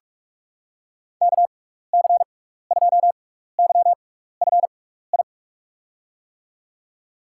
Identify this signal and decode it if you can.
Morse code, heard as a keyed tone, about 44 wpm, spaced as for 15 wpm, 700 Hz. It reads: KC2YFI
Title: USA Call Signs - 6 Characters - 6x Character Spacing - 45wpm